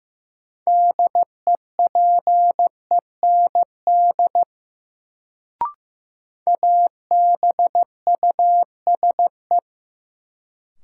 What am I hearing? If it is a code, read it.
Morse code, heard as a keyed tone, about 15 wpm, 700 Hz. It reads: DEPEND ABUSE
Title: Sentences from Top 1300 Words - New Word Drill Alt - 15wpm